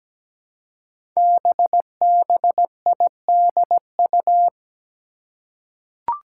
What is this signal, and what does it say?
Morse code, heard as a keyed tone, about 17 wpm, 700 Hz. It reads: BBIDU
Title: Five Challenging Letter-Number Combinations 17wpm